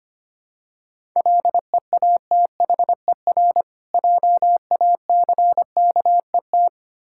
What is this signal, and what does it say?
Morse code, heard as a keyed tone, about 25 wpm, 700 Hz. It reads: LEATHER JACKET
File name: practice / 25wpm / Difficult Words with Hints - 25wpm